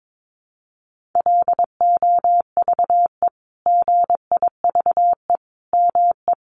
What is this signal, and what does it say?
Morse code, heard as a keyed tone, about 22 wpm, 700 Hz. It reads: LOVE GIVE ME
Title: Sentences from Top 1300 Words 22wpm